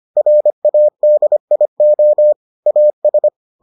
Morse code, heard as a keyed tone, about 25 wpm, 600 Hz. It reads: RADIO AS